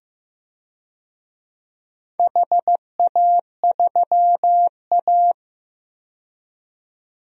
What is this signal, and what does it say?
Morse code, heard as a keyed tone, about 15 wpm, 700 Hz. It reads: HA3A
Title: DXCC-100 Call Signs - 4 Characters - 15wpm